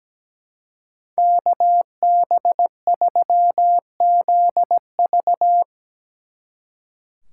Morse code, heard as a keyed tone, about 17 wpm, 700 Hz. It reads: KB3ZV
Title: Speed Racing - USA 2x2 Call Signs 17wpm